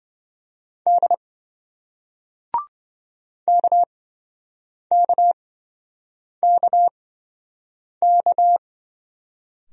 Morse code, heard as a keyed tone, about 28 wpm, spaced as for 7 wpm, 700 Hz. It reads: D KKKK